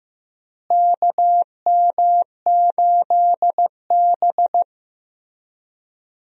Morse code, heard as a keyed tone, about 15 wpm, 700 Hz. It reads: KM8B
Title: USA Call Signs - 4 Characters - Repeated 2x - 15wpm